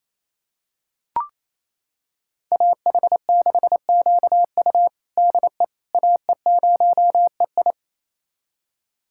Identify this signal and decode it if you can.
Morse code, heard as a keyed tone, about 28 wpm, 700 Hz. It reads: AH6QU DE AE0EI
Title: USA Callsign DE Callsign - 5 Characters - 28wpm